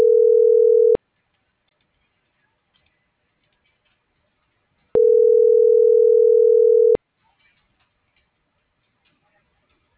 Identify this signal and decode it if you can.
Ringback tone (North America), 440 + 480 Hz.